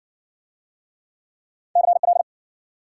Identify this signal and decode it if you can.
Morse code, heard as a keyed tone, about 57 wpm, 700 Hz. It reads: CL